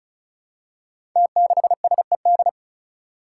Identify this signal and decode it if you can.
Morse code, heard as a keyed tone, about 36 wpm, 700 Hz. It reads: T6SED
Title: Five Challenging Letter-Number Combinations 35wpm